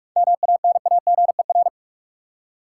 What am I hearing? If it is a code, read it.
Morse code, heard as a keyed tone, about 46 wpm, 700 Hz. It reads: MANAGER
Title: Seven Letter Words - 45wpm